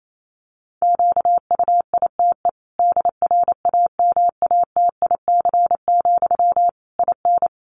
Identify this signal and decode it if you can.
Morse code, heard as a keyed tone, about 28 wpm, 700 Hz. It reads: QUITE DRAMATIC, IN
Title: Sentences from Top 1300 Words 28wpm